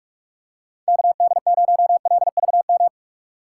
Morse code, heard as a keyed tone, about 46 wpm, 700 Hz. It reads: KD0LVM
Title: USA Call Signs - 6 Characters - 6x Character Spacing - 45wpm